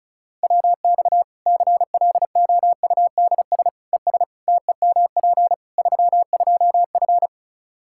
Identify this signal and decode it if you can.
Morse code, heard as a keyed tone, about 36 wpm, 700 Hz. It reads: WX CLOUDS ES TEMP 32F
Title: Practice QSO - Exchange 2 - Info - 35wpm